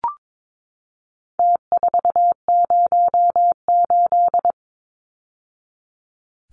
Morse code, heard as a keyed tone, about 22 wpm, 700 Hz. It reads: T408
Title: Numbers - Contest Style - 4-Digit 22wpm